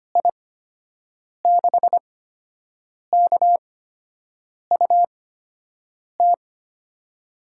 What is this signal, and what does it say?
Morse code, heard as a keyed tone, about 25 wpm, spaced as for 7 wpm, 700 Hz. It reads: I6KUT
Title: USA Call Signs - 6 Characters - 8x Character Spacing - 25wpm